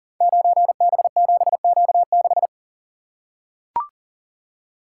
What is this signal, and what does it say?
Morse code, heard as a keyed tone, about 40 wpm, 700 Hz. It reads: 9B7Q6
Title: Code Groups - 5 Letter:Numbers - 40wpm